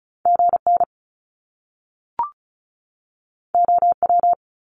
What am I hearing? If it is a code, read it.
Morse code, heard as a keyed tone, about 36 wpm, 700 Hz. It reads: GN OW
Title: Top Two Letter Combinations in Words 35wpm